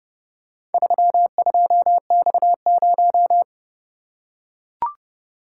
Morse code, heard as a keyed tone, about 29 wpm, 700 Hz. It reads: 32X0